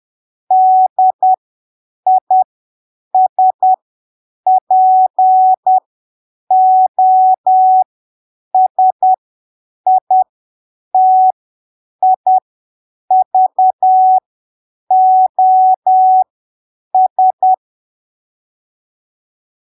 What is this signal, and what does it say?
Morse code, heard as a keyed tone, about 10 wpm, spaced as for 7 wpm, 750 Hz. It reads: DISPOSITIVOS
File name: arquivos / texto / texto 03 10ppm